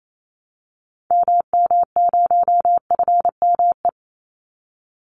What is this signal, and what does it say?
Morse code, heard as a keyed tone, about 28 wpm, 700 Hz. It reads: MM0FME
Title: Call Signs - 28wpm